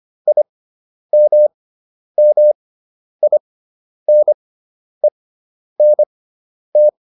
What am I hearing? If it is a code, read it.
Morse code, heard as a keyed tone, about 25 wpm, spaced as for 10 wpm, 600 Hz. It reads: IMMINENT